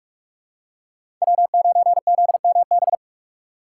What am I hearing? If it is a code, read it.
Morse code, heard as a keyed tone, about 44 wpm, 690 Hz. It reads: W9ZMB
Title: USA Call Signs - 4x Character Spacing - 45wpm